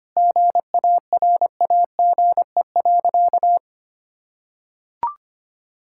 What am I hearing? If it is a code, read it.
Morse code, heard as a keyed tone, about 25 wpm, 700 Hz. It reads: GARAGE.
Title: Sentences from Top 1600 Words 25wpm